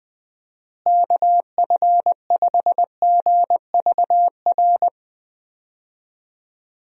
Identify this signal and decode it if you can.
Morse code, heard as a keyed tone, about 20 wpm, 700 Hz. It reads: KF5GVR
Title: USA Call Signs - 6 Characters - Repeated 3x - 20wpm